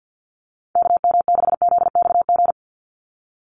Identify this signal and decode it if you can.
Morse code, heard as a keyed tone, about 48 wpm, 700 Hz. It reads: KM6ZXG
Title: USA Call Signs 50wpm